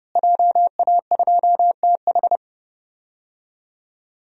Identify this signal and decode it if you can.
Morse code, heard as a keyed tone, about 30 wpm, 700 Hz. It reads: JA2TH
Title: DXCC-100 Call Signs - 5 Characters - Repeated 3x - 30wpm